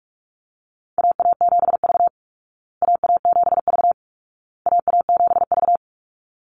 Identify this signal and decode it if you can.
Morse code, heard as a keyed tone, about 44 wpm, 700 Hz. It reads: AA7V AA7V AA7V
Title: Call Signs Repeated 2x - 45wpm